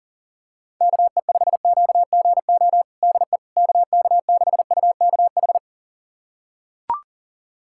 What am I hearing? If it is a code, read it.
Morse code, heard as a keyed tone, about 40 wpm, 700 Hz. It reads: KE5QGO DE KK6UKH
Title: USA Callsign DE Callsign - 6 Characters - 40wpm